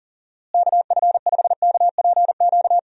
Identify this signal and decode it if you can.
Morse code, heard as a keyed tone, about 40 wpm, 700 Hz. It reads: KF5KPQ
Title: USA Call Signs - 6 Characters - Repeated 2x - 40wpm